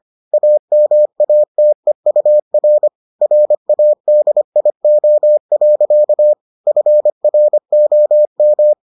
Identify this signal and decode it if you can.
Morse code, heard as a keyed tone, about 25 wpm, 600 Hz. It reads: AMATEUR RADIO. FROM